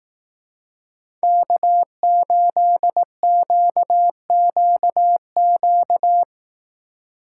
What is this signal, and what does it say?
Morse code, heard as a keyed tone, about 18 wpm, 700 Hz. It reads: K8QQQ
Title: Speed Racing - USA 1x3 Call Signs 15wpm